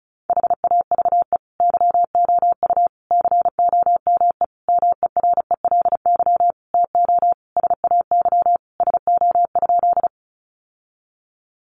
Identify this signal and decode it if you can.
Morse code, heard as a keyed tone, about 34 wpm, 700 Hz. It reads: HAVE YOU COME MERELY TO SAY SO?
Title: Sentences from Top 1400 Words 35wpm